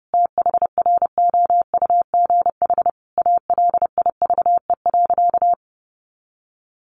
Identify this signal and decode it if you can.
Morse code, heard as a keyed tone, about 30 wpm, 700 Hz. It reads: THROUGH ALIVE.
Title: A Sound Of Thunder 30wpm